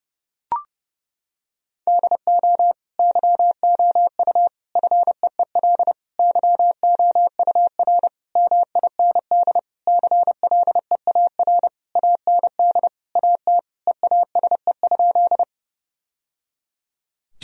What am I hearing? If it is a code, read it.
Morse code, heard as a keyed tone, about 30 wpm, 700 Hz. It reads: DO YOU FEEL YOUR MIND CLEAR AND AT EASE?
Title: Sentences from Top 1000 Words - Review Part 1 - 30wpm